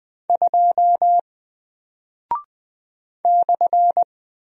Morse code, heard as a keyed tone, about 20 wpm, 700 Hz. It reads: 2 /
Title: Single Character (Letter:Number:Punctuation:Prosign) Alt 20wpm